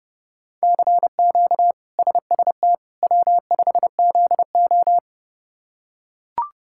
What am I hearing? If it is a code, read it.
Morse code, heard as a keyed tone, about 30 wpm, 700 Hz. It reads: CQ SST W5ZO
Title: K1USN SST QSO Practice - 30wpm